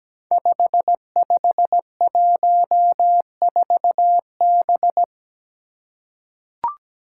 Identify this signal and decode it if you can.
Morse code, heard as a keyed tone, about 17 wpm, 700 Hz. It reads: 5514B
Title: Five Challenging Letter-Number Combinations 17wpm